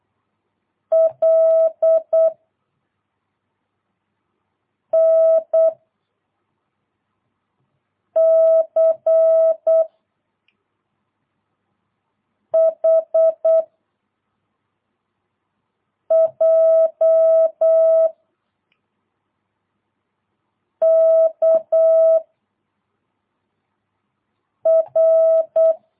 Morse code, heard as a keyed tone, about 8 wpm, 660 Hz. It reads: L N C H J K R
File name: jh8cbh / 2022cw